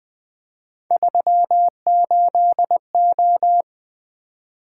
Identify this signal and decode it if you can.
Morse code, heard as a keyed tone, about 20 wpm, 700 Hz. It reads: 38O
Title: Three Letter-Number Combinations - 20wpm